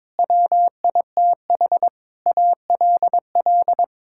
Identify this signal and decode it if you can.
Morse code, heard as a keyed tone, about 22 wpm, 700 Hz. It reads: WITH ALL
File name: Sentences with 9-Word Phrases - 22wpm